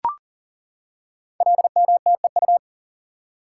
Morse code, heard as a keyed tone, about 38 wpm, 690 Hz. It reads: LMTEU
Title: Code Groups - 5 Letters - 40wpm